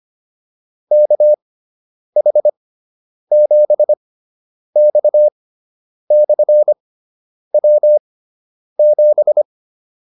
Morse code, heard as a keyed tone, about 25 wpm, spaced as for 9 wpm, 600 Hz. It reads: KH7X/W7